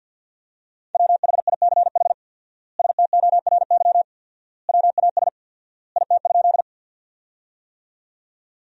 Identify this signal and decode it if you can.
Morse code, heard as a keyed tone, about 53 wpm, 700 Hz. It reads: WHICH STORY WAS IT?
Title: 4x Word-Spacing - Sentences from Top 300 Words - 50wpm